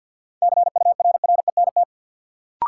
Morse code, heard as a keyed tone, about 47 wpm, 700 Hz. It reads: CURRENT
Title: Sentences from Top 1000 Words - Review Part 4 - New Word Drill - 50wpm